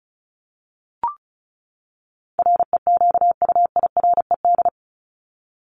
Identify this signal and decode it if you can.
Morse code, heard as a keyed tone, about 34 wpm, 700 Hz. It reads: REQUIRED
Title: ED Suffix - Top 100 Words with Base Words 35wpm